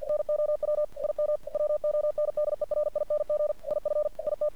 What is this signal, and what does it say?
Morse code, heard as a keyed tone, about 47 wpm, 600 Hz. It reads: NOW IM WONDERING IF IT
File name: archived / CRSnet